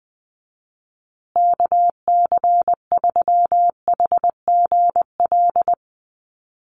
Morse code, heard as a keyed tone, about 20 wpm, 700 Hz. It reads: KC3HGL
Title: USA Call Signs Repeated 2x 20wpm